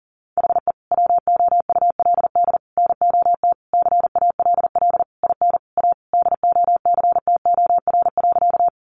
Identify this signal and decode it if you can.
Morse code, heard as a keyed tone, about 40 wpm, 700 Hz. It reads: HE WOULD NOT CALL IN A DOCTOR.
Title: Sentences from Top 1500 Words 40wpm